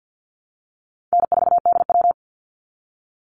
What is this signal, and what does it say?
Morse code, heard as a keyed tone, about 48 wpm, 700 Hz. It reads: N4DW
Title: Easy Call Signs Repeated 2x - 50wpm